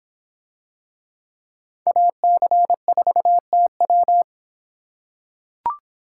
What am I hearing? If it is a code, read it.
Morse code, heard as a keyed tone, about 26 wpm, 700 Hz. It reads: AC4TW